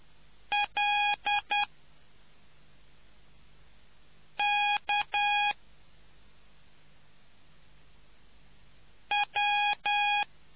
Morse code, heard as a keyed tone, about 10 wpm, 820 Hz. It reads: L K W